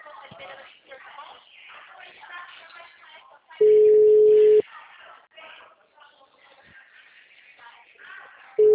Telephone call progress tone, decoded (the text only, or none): ringback tone